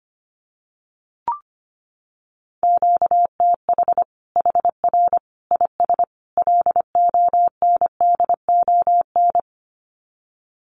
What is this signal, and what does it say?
Morse code, heard as a keyed tone, about 25 wpm, 700 Hz. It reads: QTH HR IS LONDON
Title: Common QSO Phrases 25wpm